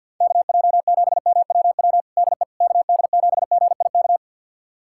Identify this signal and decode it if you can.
Morse code, heard as a keyed tone, about 49 wpm, 700 Hz. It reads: KJ7MWW DE KD7GIK
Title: USA Callsign DE Callsign - 6 Characters - 50wpm